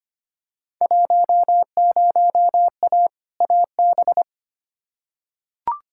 Morse code, heard as a keyed tone, about 25 wpm, 700 Hz. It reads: 10A AB
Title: Field Day Exchanges Encore - 25wpm